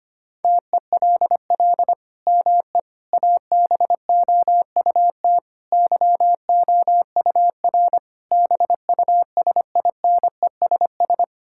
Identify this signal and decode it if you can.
Morse code, heard as a keyed tone, about 25 wpm, 700 Hz. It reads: TELL ME ABOUT YOUR BUSINESS
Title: Sentences from Top 1000 Words - Review Part 5 - 25wpm